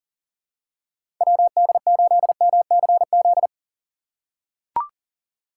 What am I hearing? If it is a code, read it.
Morse code, heard as a keyed tone, about 40 wpm, 700 Hz. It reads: WD8MCZ
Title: USA Call Signs - 6 Characters - 6x Character Spacing - 40wpm